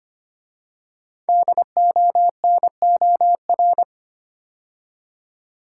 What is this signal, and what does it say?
Morse code, heard as a keyed tone, about 25 wpm, 700 Hz. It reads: DONOR
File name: 5 Letter Words - First 3 Letters Given - 25wpm